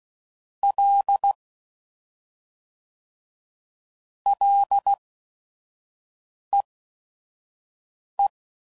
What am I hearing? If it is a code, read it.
Morse code, heard as a keyed tone, about 16 wpm, spaced as for 5 wpm, 800 Hz. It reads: L LEE